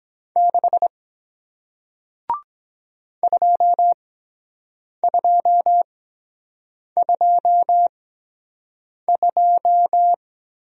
Morse code, heard as a keyed tone, about 23 wpm, 700 Hz. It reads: 6 2 2 2 2